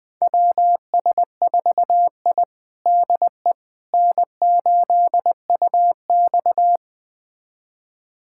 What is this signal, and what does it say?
Morse code, heard as a keyed tone, about 20 wpm, 700 Hz. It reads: WS4I DE N8UX
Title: USA Callsign DE Callsign - 4 Characters - 20wpm